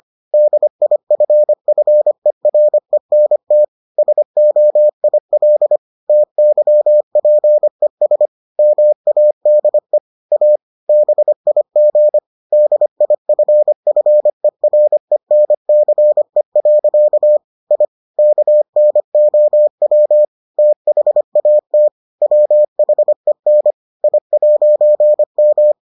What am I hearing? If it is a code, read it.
Morse code, heard as a keyed tone, about 25 wpm, 600 Hz. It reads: DIFFERENT SOIL TYPES MADE A BIG DIFFERENCE. I KNOW THAT WHEN I'M